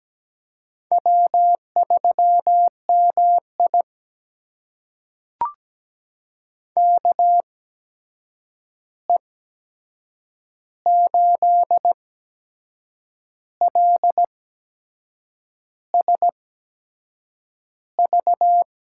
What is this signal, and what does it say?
Morse code, heard as a keyed tone, about 17 wpm, spaced as for 6 wpm, 700 Hz. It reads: W3MI KE8LSV
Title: USA Call Signs - 8x Character Spacing - 17wpm